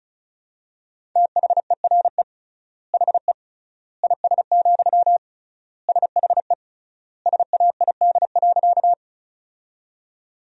Morse code, heard as a keyed tone, about 34 wpm, 700 Hz. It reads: THERE HE IS, SHE SAID.
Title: 3x Word-Spacing - Sentences from Top 100 Words - 35wpm